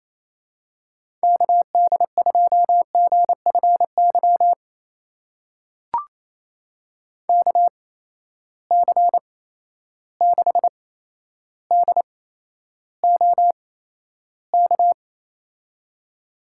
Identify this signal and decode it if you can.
Morse code, heard as a keyed tone, about 28 wpm, spaced as for 12 wpm, 700 Hz. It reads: KD2GFY KC6DOK
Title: USA Call Signs - 6 Characters - 8x Character Spacing - 28wpm